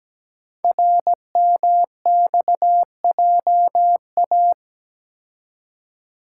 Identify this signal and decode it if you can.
Morse code, heard as a keyed tone, about 17 wpm, 700 Hz. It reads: RMXJA
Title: Code Groups - 5 Letters - 17wpm